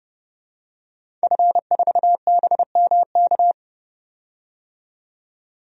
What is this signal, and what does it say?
Morse code, heard as a keyed tone, about 32 wpm, 700 Hz. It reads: F4BMK
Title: DXCC-100 Call Signs - 5 Characters - 30wpm